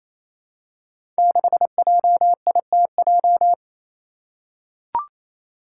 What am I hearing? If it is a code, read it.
Morse code, heard as a keyed tone, about 28 wpm, 700 Hz. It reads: 6JITJ
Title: Five Challenging Letter-Number Combinations 28wpm